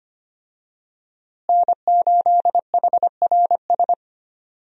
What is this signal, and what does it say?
Morse code, heard as a keyed tone, about 25 wpm, 700 Hz. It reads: N8HRS
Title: USA Call Signs - 5 Characters - Repeated 3x - 25wpm